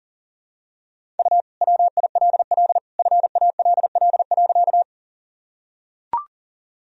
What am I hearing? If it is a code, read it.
Morse code, heard as a keyed tone, about 40 wpm, 700 Hz. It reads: U WILL FALL.
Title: Sentences from Top 500 Words and Abbreviations 40wpm